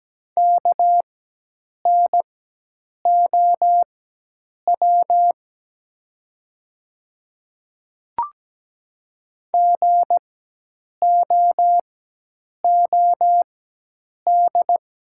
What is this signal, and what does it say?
Morse code, heard as a keyed tone, about 17 wpm, spaced as for 8 wpm, 700 Hz. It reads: KNOW GOOD
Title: 4x Character Spacing - Top 200 Words - 17wpm